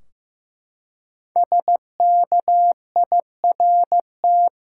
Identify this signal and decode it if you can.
Morse code, heard as a keyed tone, about 15 wpm, 700 Hz. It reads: SKIRT
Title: Five Letter Words - 15wpm